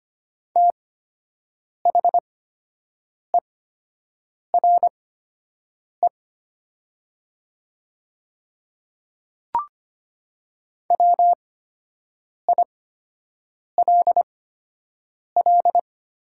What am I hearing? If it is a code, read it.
Morse code, heard as a keyed tone, about 25 wpm, spaced as for 7 wpm, 700 Hz. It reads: THERE WILL
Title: 8x Character Spacing - Top 200 Words - 25wpm